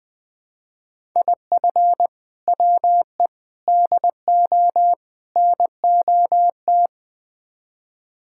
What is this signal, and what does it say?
Morse code, heard as a keyed tone, about 20 wpm, 700 Hz. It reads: IF WE DO NOT
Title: Build A Phrase - 4-Word Phrases - 20wpm